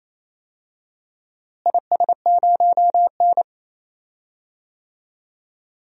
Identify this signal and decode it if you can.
Morse code, heard as a keyed tone, about 28 wpm, 700 Hz. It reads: IS0N
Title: DXCC-100 Call Signs - 4 Characters - Repeated 2x - 28wpm